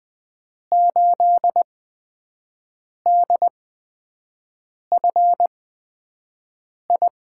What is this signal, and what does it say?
Morse code, heard as a keyed tone, about 20 wpm, spaced as for 5 wpm, 700 Hz. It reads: 8DFI
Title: USA Call Signs - 5 Characters - 8x Character Spacing - 20wpm